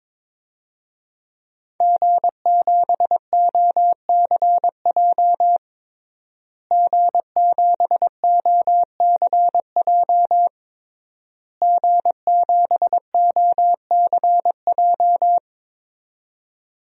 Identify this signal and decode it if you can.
Morse code, heard as a keyed tone, about 22 wpm, 700 Hz. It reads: G7OCJ G7OCJ G7OCJ
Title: DXCC-100 Call Signs - 5 Characters - Repeated 3x - 22wpm